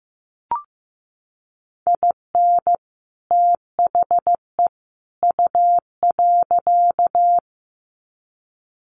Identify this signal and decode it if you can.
Morse code, heard as a keyed tone, about 15 wpm, 700 Hz. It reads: IN THE U.
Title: Sets of 3 Words 15wpm